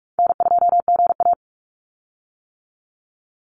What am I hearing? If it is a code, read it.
Morse code, heard as a keyed tone, about 44 wpm, 700 Hz. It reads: N2GA
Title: Call Signs - 45wpm